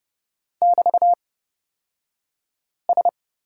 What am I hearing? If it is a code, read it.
Morse code, heard as a keyed tone, about 30 wpm, spaced as for 5 wpm, 690 Hz. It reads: <BT>S
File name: Lesson 36d - BT - Two Characters 30wpm